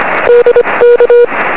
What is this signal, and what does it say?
Morse code, heard as a keyed tone, about 25 wpm, 500 Hz. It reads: DK
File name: 0544z yl2ao 23cm